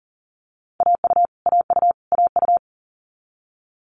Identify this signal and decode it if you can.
Morse code, heard as a keyed tone, about 37 wpm, 690 Hz. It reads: AU AU AU
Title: Multi-letter Phonemes Repeated 3x 40wpm